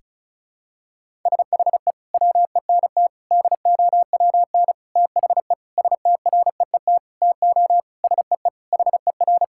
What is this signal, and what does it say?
Morse code, heard as a keyed tone, about 36 wpm, 700 Hz. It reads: SHE WENT DOWN THE STREET TO SEE HER
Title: Sentences from Top 1000 Words - Review Part 5 - 35wpm